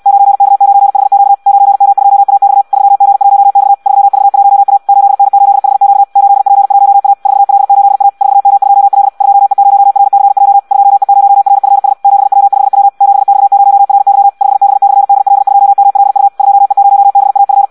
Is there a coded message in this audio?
Yes: Morse code from 0.1 to 17.6 s, heard as a keyed tone, about 74 wpm, 800 Hz. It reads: 9K9HQ 9M2AX 3G1X 3V1A 8N8HQ 8P1A 4L8A 4M2L 2E0RCJ 2E0RFS 7W5W 7X0RY 5B/DL2MLU 2E0BAF